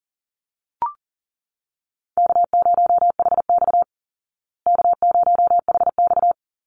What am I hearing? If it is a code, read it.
Morse code, heard as a keyed tone, about 40 wpm, 700 Hz. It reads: K0HX K0HX
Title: Easy Call Signs Repeated 2x - 40wpm